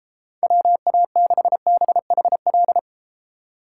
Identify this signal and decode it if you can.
Morse code, heard as a keyed tone, about 33 wpm, 700 Hz. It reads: WA6BHL